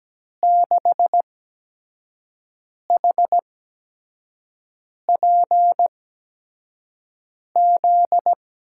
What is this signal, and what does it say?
Morse code, heard as a keyed tone, about 17 wpm, spaced as for 5 wpm, 700 Hz. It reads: 6HPZ
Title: USA Call Signs - 6 Characters - 8x Character Spacing - 17wpm